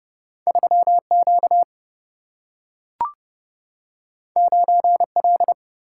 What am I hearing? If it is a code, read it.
Morse code, heard as a keyed tone, about 30 wpm, 700 Hz. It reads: 3 Q 9 L
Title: Single-Letter Single-Number 30wpm